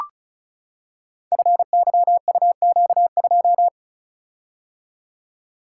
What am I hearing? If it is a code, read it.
Morse code, heard as a keyed tone, about 34 wpm, 700 Hz. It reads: FYUQ2
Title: Code Groups - 5 Letter:Numbers - 35wpm